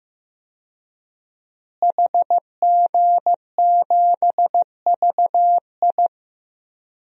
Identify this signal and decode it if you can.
Morse code, heard as a keyed tone, about 15 wpm, 700 Hz. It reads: HG7VI
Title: DXCC-100 Call Signs - 5 Characters - Repeated 3x - 15wpm